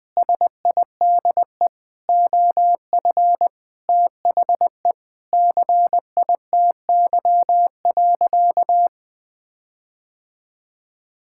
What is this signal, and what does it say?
Morse code, heard as a keyed tone, about 20 wpm, 700 Hz. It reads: SIDE OF THE CITY.
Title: Sentences with 6-Word Phrases - 20wpm